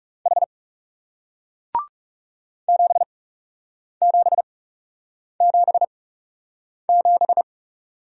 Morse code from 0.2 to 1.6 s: H

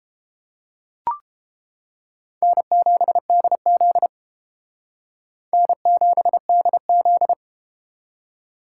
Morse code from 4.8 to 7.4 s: N7DZ

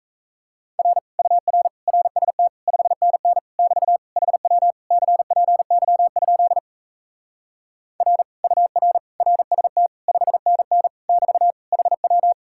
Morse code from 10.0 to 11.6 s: 5NN <BT>

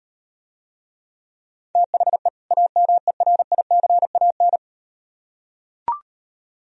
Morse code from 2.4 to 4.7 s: AMERICAN

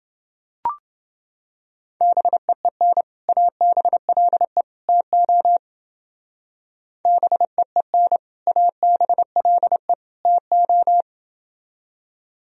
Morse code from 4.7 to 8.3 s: TO BEEN

